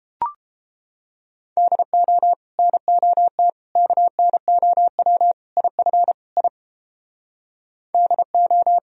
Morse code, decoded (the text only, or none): DO NOT KNOW IF I DO